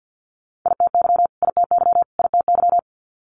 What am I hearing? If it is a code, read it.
Morse code, heard as a keyed tone, about 50 wpm, 690 Hz. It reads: ITY ITY ITY
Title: Multi-letter Phonemes Repeated 3x 50wpm